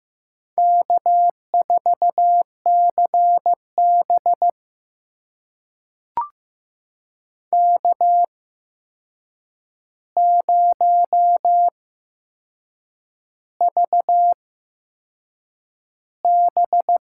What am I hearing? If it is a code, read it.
Morse code, heard as a keyed tone, about 15 wpm, spaced as for 6 wpm, 700 Hz. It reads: K4CB K0VB
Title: USA Call Signs - 4 Characters - 8x Character Spacing - 15wpm